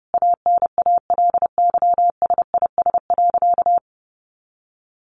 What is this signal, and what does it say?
Morse code, heard as a keyed tone, about 30 wpm, 700 Hz. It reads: ANALYSIS.
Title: Sentences from Top 1400 Words 30wpm